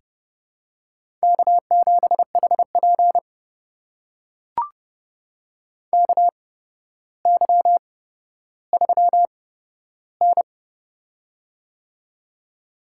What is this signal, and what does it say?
Morse code, heard as a keyed tone, about 30 wpm, spaced as for 13 wpm, 700 Hz. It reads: K7HP KY3N
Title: USA Call Signs - 4 Characters - 8x Character Spacing - 30wpm